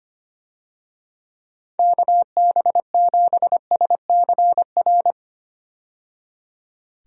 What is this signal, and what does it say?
Morse code, heard as a keyed tone, about 25 wpm, 700 Hz. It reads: KB7SCR
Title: USA Call Signs - 6 Characters - 25wpm